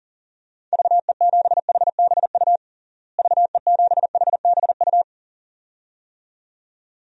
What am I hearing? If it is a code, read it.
Morse code, heard as a keyed tone, about 40 wpm, 700 Hz. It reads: VE7HBU VE7HBU
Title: DXCC-100 Call Signs - 6 Characters - Repeated 2x - 40wpm